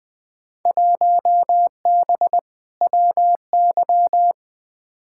Morse code, heard as a keyed tone, about 20 wpm, 700 Hz. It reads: 1B WY
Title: Field Day Exchanges Encore - 20wpm